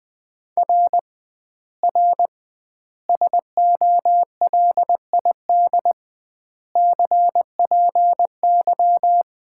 Morse code, heard as a keyed tone, about 20 wpm, 700 Hz. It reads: R R SOLID CPY
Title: Common QSO Phrases - 2x Word-Spacing - 20wpm